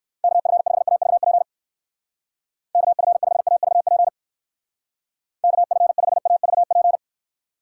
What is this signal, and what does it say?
Morse code, heard as a keyed tone, about 64 wpm, 700 Hz. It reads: KF5AVP KF5AVP KF5AVP